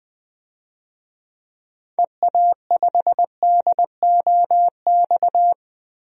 Morse code, heard as a keyed tone, about 20 wpm, 700 Hz. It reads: EA5DOX
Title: DXCC-100 Call Signs - 6 Characters - Repeated 2x - 20wpm